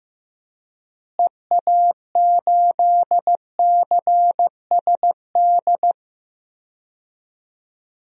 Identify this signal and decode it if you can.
Morse code, heard as a keyed tone, about 14 wpm, 700 Hz. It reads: EA8CSD
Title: DXCC-100 Call Signs - 6 Characters - Repeated 3x - 15wpm